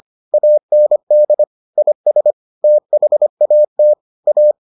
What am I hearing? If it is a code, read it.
Morse code, heard as a keyed tone, about 25 wpm, 600 Hz. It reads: AND IS THAT A